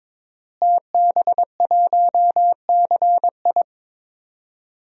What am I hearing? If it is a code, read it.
Morse code, heard as a keyed tone, about 22 wpm, 700 Hz. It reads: TB1CI
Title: DXCC-100 Call Signs - 5 Characters - Repeated 3x - 22wpm